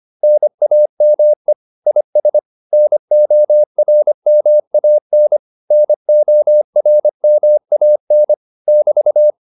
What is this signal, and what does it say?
Morse code, heard as a keyed tone, about 25 wpm, 600 Hz. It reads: NAME IS NORMAN NORMAN <BT>
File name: QSO209a 25